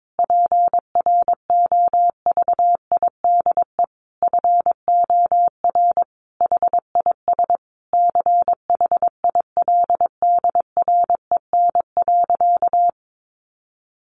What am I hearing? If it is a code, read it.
Morse code, heard as a keyed tone, about 22 wpm, 700 Hz. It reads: PROVIDE FOR HIS CHILDREN.
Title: Sentences from Top 1100 Words 22wpm